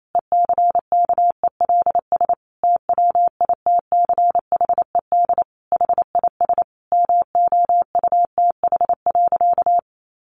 Morse code, heard as a keyed tone, about 28 wpm, 700 Hz. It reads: ECKELS TWITCHED HIS MOUTH.
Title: A Sound Of Thunder 28wpm